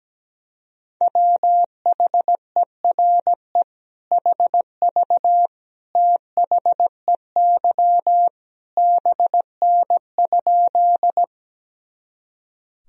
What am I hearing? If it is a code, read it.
Morse code, heard as a keyed tone, about 17 wpm, 700 Hz. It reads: WHERE HV THEY BN?
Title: Sentences from Top 200 Words and Abbreviations 17wpm